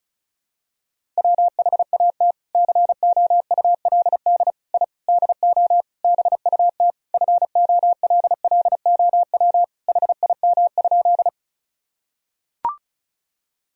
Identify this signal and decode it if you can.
Morse code, heard as a keyed tone, about 34 wpm, 700 Hz. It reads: WHAT COULD I DO BUT FOLLOW HIM?
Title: Sentences from Top 1000 Words - Review Part 1 - 35wpm